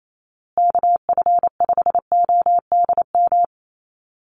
Morse code, heard as a keyed tone, about 28 wpm, 700 Hz. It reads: KF5ODM
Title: USA Call Signs Repeated 3x 28wpm